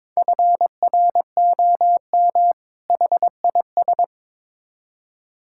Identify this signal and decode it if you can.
Morse code, heard as a keyed tone, about 22 wpm, 700 Hz. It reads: FROM HIS
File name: Build A Phrase - 2-Word Phrases - 22wpm